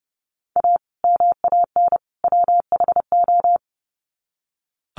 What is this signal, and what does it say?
Morse code, heard as a keyed tone, about 30 wpm, 700 Hz. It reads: A MAN WHO
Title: Sets of 3 Words 30wpm